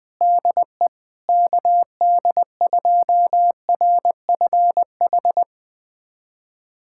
Morse code, heard as a keyed tone, about 19 wpm, 700 Hz. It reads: DE KD2RFH
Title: USA Callsign DE Callsign 20wpm